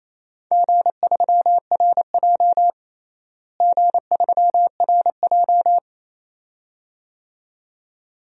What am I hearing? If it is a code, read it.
Morse code, heard as a keyed tone, about 27 wpm, 700 Hz. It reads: G3RJ G3RJ
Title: DXCC-100 Call Signs - 4 Characters - Repeated 2x - 28wpm